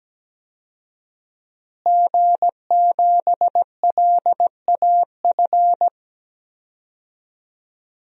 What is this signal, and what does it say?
Morse code, heard as a keyed tone, about 17 wpm, 700 Hz. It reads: G7LAF
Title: DXCC-100 Call Signs - 5 Characters - Repeated 2x - 17wpm